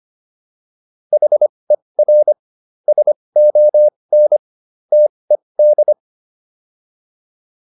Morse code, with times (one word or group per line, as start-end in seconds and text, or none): 1.1-2.3 HER
2.9-4.4 SON
4.9-5.9 TED